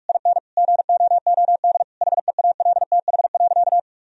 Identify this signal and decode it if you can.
Morse code, heard as a keyed tone, about 46 wpm, 700 Hz. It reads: IN GOOD HEALTH.
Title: Sentences from Top 1000 Words - Review Part 3 - 45wpm